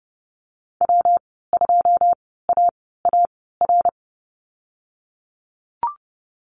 Morse code, heard as a keyed tone, about 30 wpm, spaced as for 17 wpm, 700 Hz. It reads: W2AAR
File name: Lesson 19g - Letter w - Call Signs 30wpm